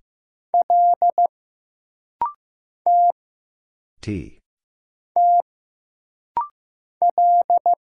Morse code, heard as a keyed tone, about 15 wpm, 700 Hz. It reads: L T T L
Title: Single Letter-Number Alt - Rapid-Fire 15wpm